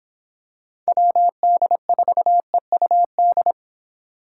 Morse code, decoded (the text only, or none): WD4EUD